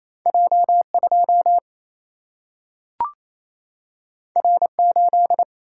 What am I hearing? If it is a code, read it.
Morse code, heard as a keyed tone, about 28 wpm, 700 Hz. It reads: J2 R8
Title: Single-Letter Single-Number 28wpm